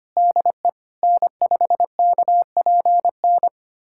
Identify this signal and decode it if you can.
Morse code, heard as a keyed tone, about 25 wpm, 700 Hz. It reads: DE N5KPN
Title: USA Callsign DE Callsign - 5 Characters - 25wpm